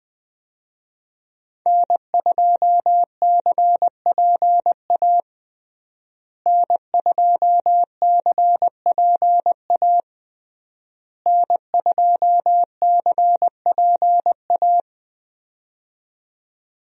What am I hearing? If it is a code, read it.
Morse code, heard as a keyed tone, about 20 wpm, 700 Hz. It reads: N2CPA N2CPA N2CPA
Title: USA Call Signs - 5 Characters - Repeated 3x - 20wpm